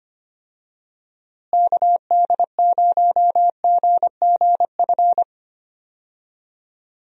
Morse code, heard as a keyed tone, about 25 wpm, 700 Hz. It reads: KD0GGF
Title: USA Call Signs - 6 Characters - Repeated 2x - 25wpm